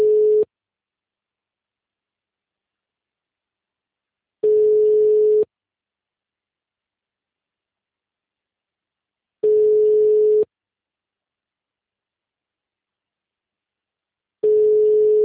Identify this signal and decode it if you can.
Ringback tone (Europe, 425 Hz).